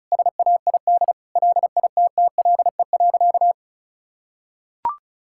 Morse code, heard as a keyed tone, about 36 wpm, 700 Hz. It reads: SAID LITTLE.
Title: Sentences from Top 1600 Words 35wpm